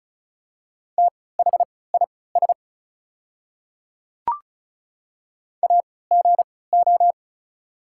Morse code, heard as a keyed tone, about 36 wpm, spaced as for 20 wpm, 690 Hz. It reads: THIS AGO
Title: 3x Character Spacing - Top 500 Words - 35wpm